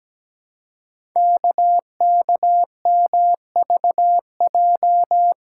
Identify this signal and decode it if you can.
Morse code, heard as a keyed tone, about 17 wpm, 700 Hz. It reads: KKMVJ
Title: Code Groups - 5 Letters - 17wpm